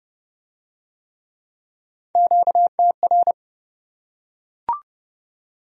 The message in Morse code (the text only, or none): QTR